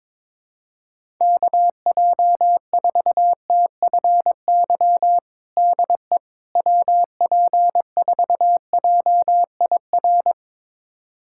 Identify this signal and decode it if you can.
Morse code, heard as a keyed tone, about 22 wpm, 700 Hz. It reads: KJ4TFY DE WP4JIR
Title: USA Callsign DE Callsign 22wpm